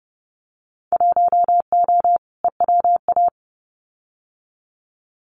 Morse code, heard as a keyed tone, about 30 wpm, 700 Hz. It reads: 1O EWA
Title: Winter Field Day Exchanges Encore 30wpm